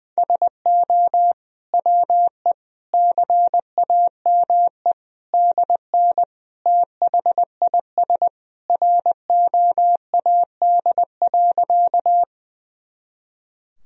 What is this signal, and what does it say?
Morse code, heard as a keyed tone, about 20 wpm, 700 Hz. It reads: SO WE CAME DN THIS ROAD.
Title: Sentences from Top 400 Words and Abbreviations 20wpm